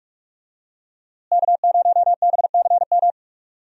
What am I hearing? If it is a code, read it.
Morse code, heard as a keyed tone, about 43 wpm, 700 Hz. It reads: K0BCM
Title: USA Call Signs - 5 Characters - Repeated 2x - 45wpm